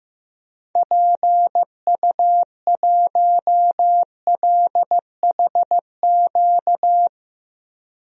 Morse code, heard as a keyed tone, about 15 wpm, 700 Hz. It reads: PU1LHQ
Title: DXCC-100 Call Signs - 6 Characters - 15wpm